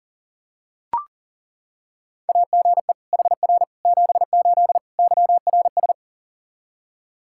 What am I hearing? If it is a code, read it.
Morse code, heard as a keyed tone, about 40 wpm, 700 Hz. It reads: AGE HR 78 YRS
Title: Practice QSO - Exchange 3 - More Info - 40wpm